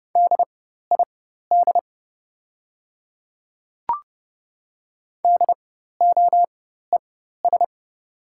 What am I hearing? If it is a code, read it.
Morse code, heard as a keyed tone, about 30 wpm, spaced as for 14 wpm, 700 Hz. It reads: DID DOES
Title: 4x Character Spacing - Top 500 Words - 30wpm